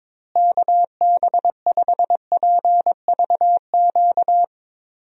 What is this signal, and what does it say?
Morse code, heard as a keyed tone, about 22 wpm, 700 Hz. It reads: KB5PVQ
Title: USA Call Signs - 6 Characters - Repeated 3x - 22wpm